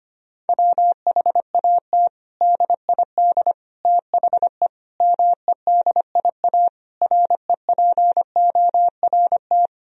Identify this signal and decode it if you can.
Morse code, heard as a keyed tone, about 25 wpm, 700 Hz. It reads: WHAT DID THE MEDIA REPORT
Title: Sentences from Top 1000 Words - Review Part 4 - 25wpm